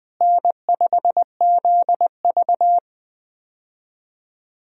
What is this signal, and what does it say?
Morse code, heard as a keyed tone, about 20 wpm, 700 Hz. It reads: N5ZV
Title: USA Call Signs - 4 Characters - Repeated 3x - 20wpm